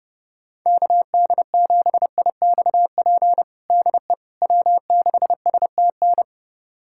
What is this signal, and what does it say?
Morse code, heard as a keyed tone, about 30 wpm, 700 Hz. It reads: KD7IXP DE W6STN
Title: USA Callsign DE Callsign 30wpm